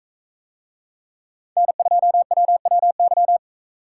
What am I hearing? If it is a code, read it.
Morse code, heard as a keyed tone, about 44 wpm, 700 Hz. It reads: N2WWY